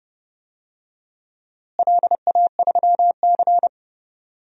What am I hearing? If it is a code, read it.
Morse code, heard as a keyed tone, about 30 wpm, 700 Hz. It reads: LA3C
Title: DXCC-100 Call Signs - 4 Characters - Repeated 2x - 30wpm